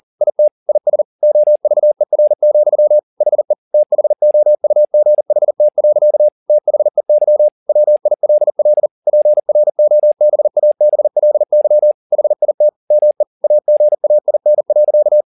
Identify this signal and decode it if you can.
Morse code, heard as a keyed tone, about 40 wpm, 600 Hz. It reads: IT IS OVER, HE THOUGHT. THEY WILL PROBABLY HIT ME AGAIN.
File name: the old man and the sea 2 40wpm